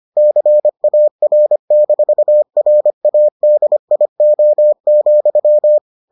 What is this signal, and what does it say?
Morse code, heard as a keyed tone, about 25 wpm, 600 Hz. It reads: CAR-RADIO,